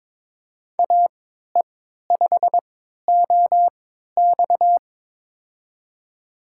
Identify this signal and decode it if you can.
Morse code, heard as a keyed tone, about 22 wpm, spaced as for 13 wpm, 700 Hz. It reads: AE5OX
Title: USA Call Signs - 3x Character Spacing - 22wpm